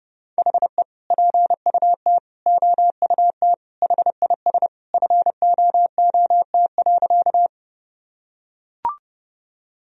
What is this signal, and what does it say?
Morse code, heard as a keyed tone, about 31 wpm, 700 Hz. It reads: HE PUT OUT HIS FOOT.
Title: Sentences from Top 1000 Words - Review Part 3 - 30wpm